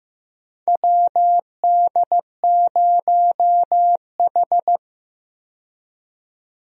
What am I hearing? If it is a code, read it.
Morse code, heard as a keyed tone, about 15 wpm, 700 Hz. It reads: WD0H
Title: USA Call Signs - 4 Characters - 15wpm